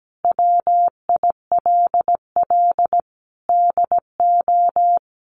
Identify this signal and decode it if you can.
Morse code, heard as a keyed tone, about 17 wpm, 700 Hz. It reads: WILL DO
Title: Sentences from Top 900 Words 17wpm